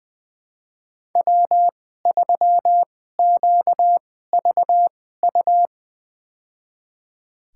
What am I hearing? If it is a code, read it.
Morse code, heard as a keyed tone, about 20 wpm, spaced as for 15 wpm, 700 Hz. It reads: W3QVU
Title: USA Call Signs - 5 Characters - 2x Character Spacing - 20wpm